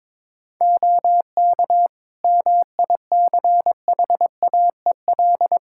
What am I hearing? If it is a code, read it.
Morse code, heard as a keyed tone, about 22 wpm, 700 Hz. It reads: OK MICHAEL
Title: K1USN SST QSO Practice 22wpm